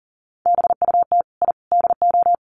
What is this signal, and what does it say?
Morse code, heard as a keyed tone, about 37 wpm, 700 Hz. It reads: BUT I DO
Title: Sets of 3 Words 40wpm